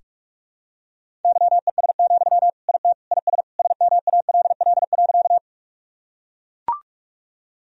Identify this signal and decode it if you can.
Morse code, heard as a keyed tone, about 41 wpm, 700 Hz. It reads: YES, IT IS SMALL.
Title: Sentences from Top 1000 Words - Review Part 1 - 45wpm